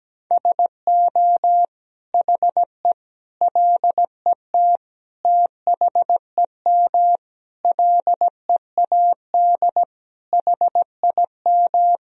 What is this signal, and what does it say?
Morse code, heard as a keyed tone, about 17 wpm, 700 Hz. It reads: SO HE LET THEM LEAD HIM
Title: Sentences from Top 1000 Words - Review Part 2 - 17wpm